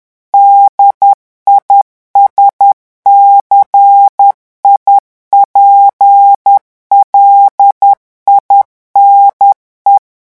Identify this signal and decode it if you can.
Morse code, heard as a keyed tone, about 10 wpm, 800 Hz. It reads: DISCIPLINE
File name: revision9 citations humoristiques vitesse 10 mots minute